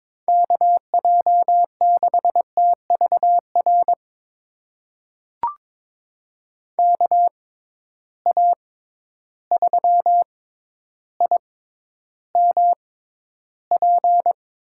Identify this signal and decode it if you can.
Morse code, heard as a keyed tone, about 22 wpm, spaced as for 11 wpm, 700 Hz. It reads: KJ6TVR KA3IMP
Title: USA Call Signs - 6 Characters - 6x Character Spacing - 22wpm